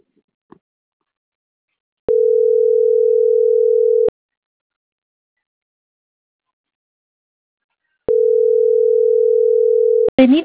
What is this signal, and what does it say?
Ringback tone (North America), 440 + 480 Hz.